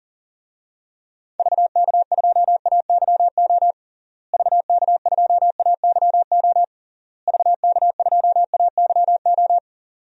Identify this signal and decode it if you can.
Morse code, heard as a keyed tone, about 40 wpm, 700 Hz. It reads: VK2AYO VK2AYO VK2AYO
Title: DXCC-100 Call Signs - 6 Characters - Repeated 3x - 40wpm